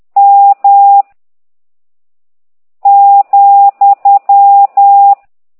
Morse code, heard as a keyed tone, about 10 wpm, 800 Hz. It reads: M ,